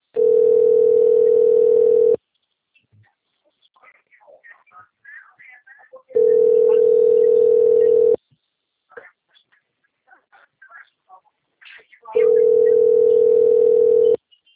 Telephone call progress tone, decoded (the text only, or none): ringback tone